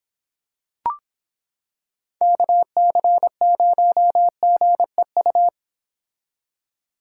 Morse code, heard as a keyed tone, about 26 wpm, 700 Hz. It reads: KC0GEU